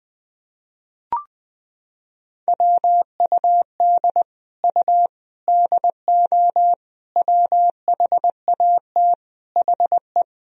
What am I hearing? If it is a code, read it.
Morse code, heard as a keyed tone, about 20 wpm, 700 Hz. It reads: WUD U DO WHAT HE
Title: Sentences from Top 100 Words and Abbreviations 20wpm